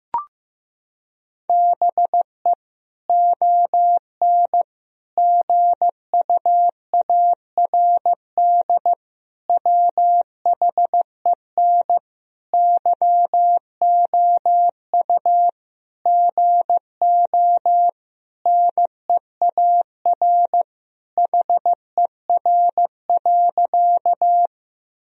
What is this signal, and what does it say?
Morse code, heard as a keyed tone, about 15 wpm, 700 Hz. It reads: BE ON GUARD WHEN YOU GO NEAR HER.
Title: Sentences from Top 1000 Words - Review Part 4 - 15wpm